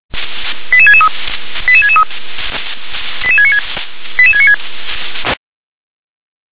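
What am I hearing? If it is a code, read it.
Five-tone selective call (ZVEI), heard as quick runs of five tones, tones of about 70 ms. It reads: 88663 88663 88696 88696
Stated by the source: ZVEI five-tone selective call 88853 received on SDR.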